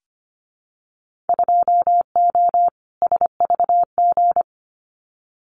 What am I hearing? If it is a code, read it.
Morse code, heard as a keyed tone, about 25 wpm, 700 Hz. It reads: 2O SVG
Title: Winter Field Day Exchanges 25wpm